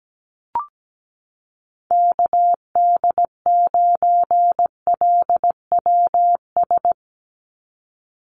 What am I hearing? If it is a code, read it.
Morse code, heard as a keyed tone, about 17 wpm, 700 Hz. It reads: KD9LWS
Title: USA Call Signs 17wpm